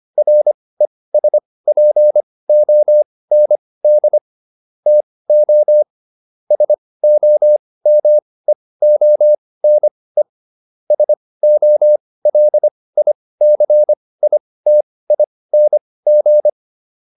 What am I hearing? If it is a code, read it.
Morse code, heard as a keyed tone, about 25 wpm, spaced as for 18 wpm, 600 Hz. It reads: RESPOND TO SOMEONE SOLICITING